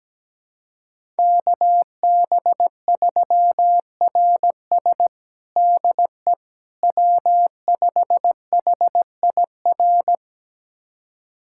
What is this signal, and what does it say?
Morse code, heard as a keyed tone, about 17 wpm, 700 Hz. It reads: KB3RS DE W5HIR
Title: USA Callsign DE Callsign 17wpm